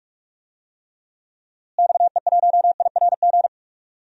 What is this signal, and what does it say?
Morse code, heard as a keyed tone, about 41 wpm, 700 Hz. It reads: XE1IRG
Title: DXCC-100 Call Signs - 6 Characters - Repeated 2x - 45wpm